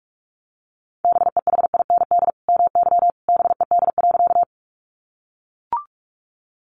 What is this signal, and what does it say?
Morse code, heard as a keyed tone, about 44 wpm, 700 Hz. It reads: BEHIND MY BED.
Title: Sentences from Top 500 Words 45wpm